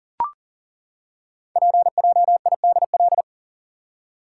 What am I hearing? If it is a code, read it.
Morse code, heard as a keyed tone, about 42 wpm, 690 Hz. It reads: PJIDL
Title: Code Groups - 5 Letters - 40wpm